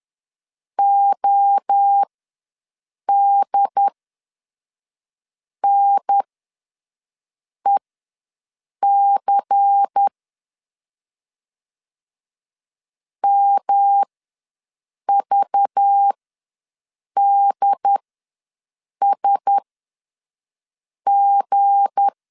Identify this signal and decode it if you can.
Morse code, heard as a keyed tone, about 10 wpm, spaced as for 5 wpm, 800 Hz. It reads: ODNEC MVDSG
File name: LEZIONI / 7 GF9Z / 7 G ws 10wpm 3843